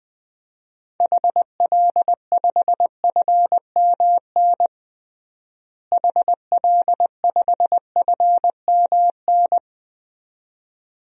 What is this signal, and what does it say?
Morse code, heard as a keyed tone, about 20 wpm, 700 Hz. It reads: HL5FMN HL5FMN
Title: DXCC-100 Call Signs - 6 Characters - Repeated 2x - 20wpm